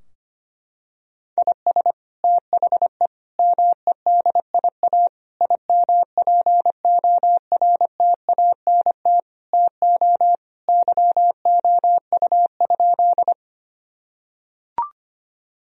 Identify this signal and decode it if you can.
Morse code, heard as a keyed tone, about 25 wpm, 700 Hz. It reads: IS THE MEDIA IMPORTANT TO YOU?
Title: Sentences from Top 1000 Words - Review Part 4 - 25wpm